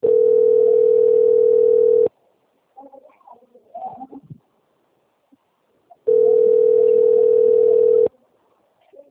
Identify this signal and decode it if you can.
Ringback tone (North America), 440 + 480 Hz.
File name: out- - - - - .10044